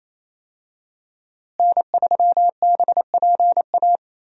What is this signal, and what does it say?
Morse code, heard as a keyed tone, about 28 wpm, 700 Hz. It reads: N3BPA
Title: USA Call Signs - 5 Characters - Repeated 2x - 28wpm